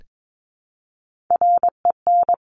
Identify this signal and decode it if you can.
Morse code, heard as a keyed tone, about 22 wpm, 700 Hz. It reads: REN
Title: Top Three Letter Combinations in Words 22wpm